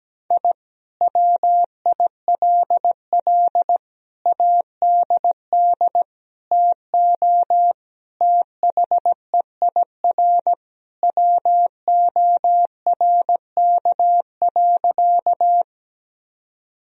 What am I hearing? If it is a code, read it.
Morse code, heard as a keyed tone, about 17 wpm, 700 Hz. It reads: I WILL ADD TO THEIR WORK.
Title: Sentences from Top 1000 Words - Review Part 1 - 17wpm